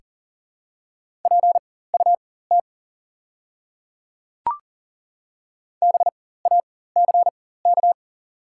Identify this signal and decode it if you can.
Morse code, heard as a keyed tone, about 40 wpm, spaced as for 19 wpm, 700 Hz. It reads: PUT BACK
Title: 4x Character Spacing - Top 500 Words - 40wpm